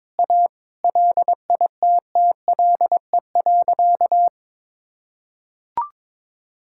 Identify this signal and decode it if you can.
Morse code, heard as a keyed tone, about 22 wpm, 700 Hz. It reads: A LITTLE.
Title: Sentences from Top 1000 Words - Review Part 5 - 22wpm